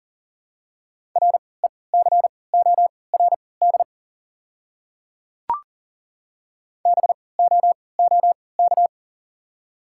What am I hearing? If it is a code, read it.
Morse code, heard as a keyed tone, about 40 wpm, spaced as for 23 wpm, 700 Hz. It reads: RECORD BOOK
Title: 3x Character Spacing - Top 500 Words - 40wpm